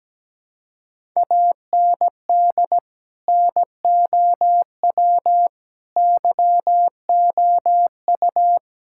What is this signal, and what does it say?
Morse code, heard as a keyed tone, about 17 wpm, 700 Hz. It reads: AND NOW YOU
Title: Sentences from Top 1000 Words - Review Part 6 - 17wpm